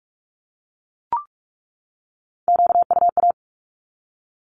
Morse code, heard as a keyed tone, about 44 wpm, 700 Hz. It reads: QUA
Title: Q-Codes 45wpm